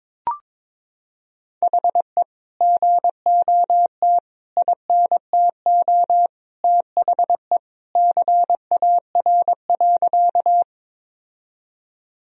Morse code, heard as a keyed tone, about 22 wpm, 700 Hz. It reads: HE GOT INTO THE CAR.
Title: Sentences from Top 1000 Words - Review Part 1 - 22wpm